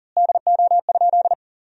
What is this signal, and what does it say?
Morse code, heard as a keyed tone, about 37 wpm, 700 Hz. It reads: DO?
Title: 3x Word-Spacing - Sentences from Top 100 Words - 40wpm